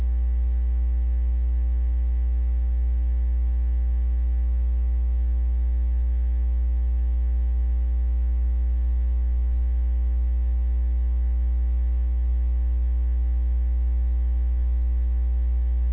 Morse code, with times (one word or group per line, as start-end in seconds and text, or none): none